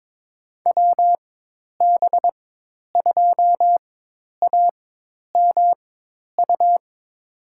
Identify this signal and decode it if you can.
Morse code, heard as a keyed tone, about 22 wpm, spaced as for 10 wpm, 700 Hz. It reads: WB2AMU
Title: USA Call Signs - 4x Character Spacing - 22wpm